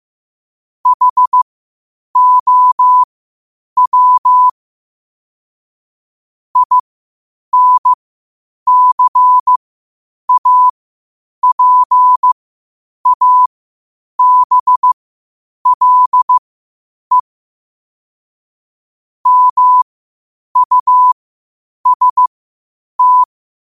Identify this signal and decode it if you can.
Morse code, heard as a keyed tone, about 15 wpm, spaced as for 8 wpm, 1000 Hz. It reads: HOW INCAPABLE MUST
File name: qotdfiles / 8wpm